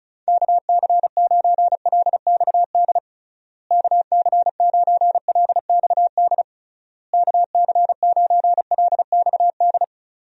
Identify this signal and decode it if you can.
Morse code, heard as a keyed tone, about 34 wpm, 700 Hz. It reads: KC9LXD KC9LXD KC9LXD
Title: USA Call Signs - 6 Characters - Repeated 3x - 35wpm